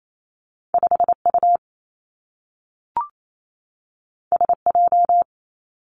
Morse code, heard as a keyed tone, about 28 wpm, 700 Hz. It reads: 5 U S J
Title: Two Challenging Letter-Number Combinations 28wpm